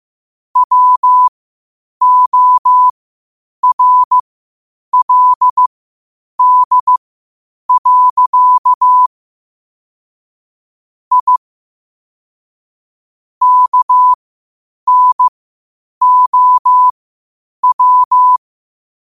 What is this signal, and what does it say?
Morse code, heard as a keyed tone, about 15 wpm, spaced as for 8 wpm, 1000 Hz. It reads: WORLD. I KNOW
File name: qotdfiles / 8wpm